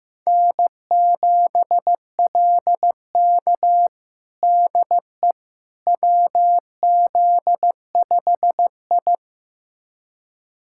Morse code, heard as a keyed tone, about 15 wpm, 700 Hz. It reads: N7LK DE WZ5I
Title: USA Callsign DE Callsign - 4 Characters - 15wpm